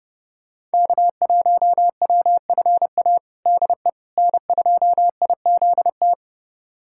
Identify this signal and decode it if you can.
Morse code, heard as a keyed tone, about 30 wpm, 700 Hz. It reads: K1WFA DE N2IZT
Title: USA Callsign DE Callsign - 5 Characters - 30wpm